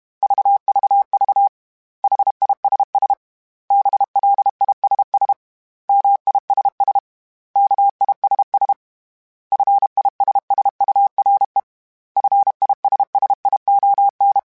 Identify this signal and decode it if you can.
Morse code, heard as a keyed tone, about 32 wpm, 800 Hz. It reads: VVV HISS BLISS MISS KISS FISSURE FISSION
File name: Special INGs-mp3s / Special INGs-mp3s / Suffixes @30